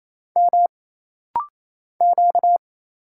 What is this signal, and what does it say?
Morse code, heard as a keyed tone, about 27 wpm, 700 Hz. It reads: M Q
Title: Single Letter-Number - Rapid-Fire 28wpm